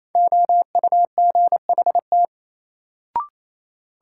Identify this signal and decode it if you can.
Morse code, heard as a keyed tone, about 28 wpm, 700 Hz. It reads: OUGHT
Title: Sets of 5 Words Encore - New Word Drill - 28wpm